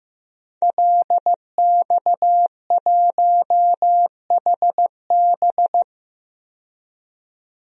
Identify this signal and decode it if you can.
Morse code, heard as a keyed tone, about 15 wpm, 700 Hz. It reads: LX1HB
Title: DXCC-100 Call Signs - 5 Characters - 15wpm